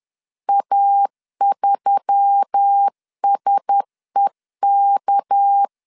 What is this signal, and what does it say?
Morse code, heard as a keyed tone, about 10 wpm, 800 Hz. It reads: A3SEK
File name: LEZIONI / 6 PKX0 / 6 PK 10wpm 2436